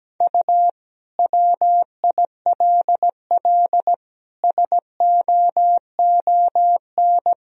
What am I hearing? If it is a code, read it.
Morse code, heard as a keyed tone, about 17 wpm, 700 Hz. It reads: U WILL SOON
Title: Sentences from Top 400 Words and Abbreviations 17wpm